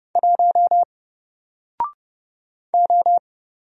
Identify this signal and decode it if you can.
Morse code, heard as a keyed tone, about 30 wpm, 700 Hz. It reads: 1 O
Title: Single Letter-Number Alt 30wpm